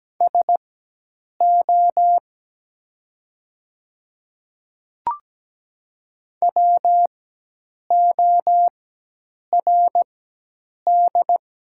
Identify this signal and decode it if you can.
Morse code, heard as a keyed tone, about 17 wpm, spaced as for 8 wpm, 700 Hz. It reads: SO WORD
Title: 4x Character Spacing - Top 100 Words - 17wpm